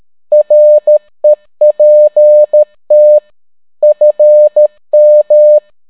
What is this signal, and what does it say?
Morse code, heard as a keyed tone, about 13 wpm, 600 Hz. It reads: REPT FM